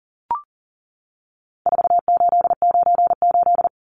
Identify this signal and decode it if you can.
Morse code, heard as a keyed tone, about 40 wpm, 700 Hz. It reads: 4898
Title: Numbers - 4-Digit 40wpm